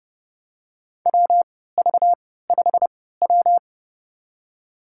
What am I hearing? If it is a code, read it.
Morse code, heard as a keyed tone, about 30 wpm, spaced as for 17 wpm, 700 Hz. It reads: WV5W
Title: USA Call Signs - 4 Characters - 3x Character Spacing - 30wpm